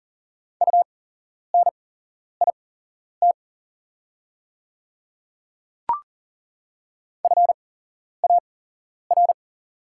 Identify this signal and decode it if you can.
Morse code, heard as a keyed tone, about 40 wpm, spaced as for 11 wpm, 700 Hz. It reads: UNIT FAR
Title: 8x Character Spacing - Top 500 Words - 40wpm